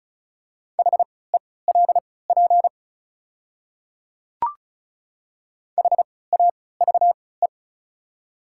Morse code, heard as a keyed tone, about 34 wpm, spaced as for 20 wpm, 700 Hz. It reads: HELP HAVE
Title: 3x Character Spacing - Top 200 Words - 35wpm